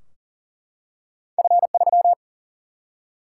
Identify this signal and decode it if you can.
Morse code, heard as a keyed tone, about 40 wpm, 700 Hz. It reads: F3
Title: Single-Letter Single-Number 40wpm